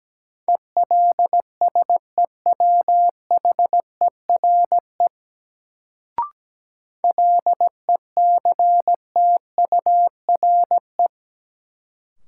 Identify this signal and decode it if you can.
Morse code, heard as a keyed tone, about 17 wpm, 700 Hz. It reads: ELSEWHERE LECTURE
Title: Sentences from Top 1600 Words - New Word Drill Alt - 17wpm